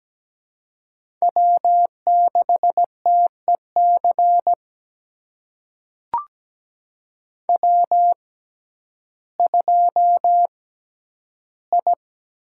Morse code, heard as a keyed tone, about 17 wpm, spaced as for 10 wpm, 700 Hz. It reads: W6TEC W2I
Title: USA Call Signs - 5 Characters - 6x Character Spacing - 17wpm